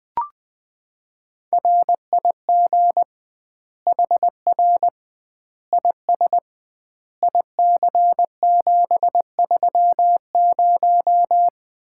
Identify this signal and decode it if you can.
Morse code, heard as a keyed tone, about 20 wpm, 700 Hz. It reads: RIG HR IS IC730
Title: Common QSO Phrases - 2x Word-Spacing - 20wpm